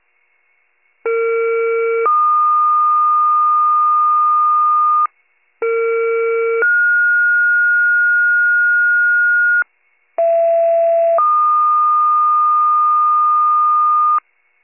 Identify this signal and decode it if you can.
Two-tone page, heard as a one-second tone, then a three-second tone, about 460 then 1220 Hz, about 460 then 1430 Hz, about 670 then 1220 Hz. It is heard from 1.0 to 14.2 s.